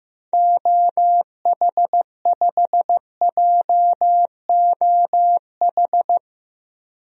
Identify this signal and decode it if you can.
Morse code, heard as a keyed tone, about 15 wpm, 700 Hz. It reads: OH5JOH
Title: DXCC-100 Call Signs - 6 Characters - Repeated 2x - 15wpm